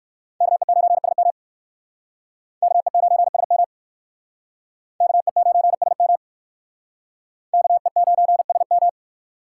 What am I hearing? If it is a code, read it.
Morse code, heard as a keyed tone, about 56 wpm, 700 Hz. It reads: KE9SM KE9SM KE9SM KE9SM